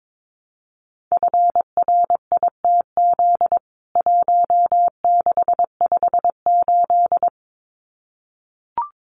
Morse code, heard as a keyed tone, about 22 wpm, 700 Hz. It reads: FRITZ 1658
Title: CWT Exchange 22wpm